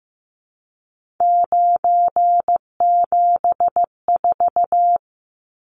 Morse code, heard as a keyed tone, about 15 wpm, 700 Hz. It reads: 974
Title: Numbers - 3-Digit 15wpm